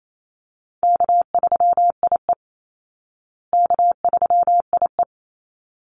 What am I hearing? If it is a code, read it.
Morse code, heard as a keyed tone, about 28 wpm, 690 Hz. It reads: K3IE K3IE
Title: Call Signs Repeated 3x - 28wpm